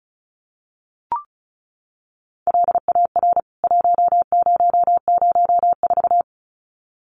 Morse code, heard as a keyed tone, about 34 wpm, 700 Hz. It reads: LAR 1004
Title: CWT Exchange 35wpm